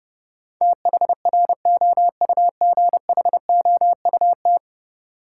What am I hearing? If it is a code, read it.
Morse code, heard as a keyed tone, about 30 wpm, 700 Hz. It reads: THROUGHOUT
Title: Sentences from Top 800 Words - New Word Drill - 30wpm